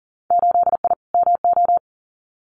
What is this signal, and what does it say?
Morse code, heard as a keyed tone, about 40 wpm, 700 Hz. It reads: 8I MO
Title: Winter Field Day Exchanges Encore 40wpm